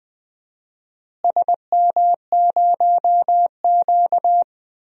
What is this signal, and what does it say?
Morse code, heard as a keyed tone, about 20 wpm, 700 Hz. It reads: SM0Q
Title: DXCC-100 Call Signs - 4 Characters - Repeated 2x - 20wpm